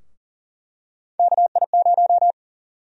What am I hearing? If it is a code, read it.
Morse code, heard as a keyed tone, about 40 wpm, 700 Hz. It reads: KI0
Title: USA Call Sign Prefixes - 40wpm